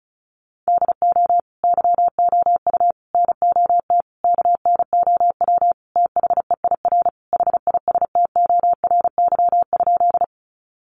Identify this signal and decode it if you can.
Morse code, heard as a keyed tone, about 34 wpm, 700 Hz. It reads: DO YOU NOT KNOW THEIR HISTORY?
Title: Sentences from Top 600 Words 35wpm